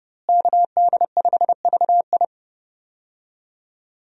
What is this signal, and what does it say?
Morse code, heard as a keyed tone, about 30 wpm, 700 Hz. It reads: KD5VI
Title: USA Call Signs - 5 Characters - Repeated 3x - 30wpm